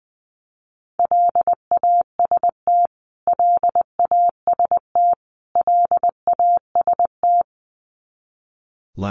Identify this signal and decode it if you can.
Morse code, heard as a keyed tone, about 20 wpm, 700 Hz. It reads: LAST LAST LAST
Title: Top 500 Words Repeated 20wpm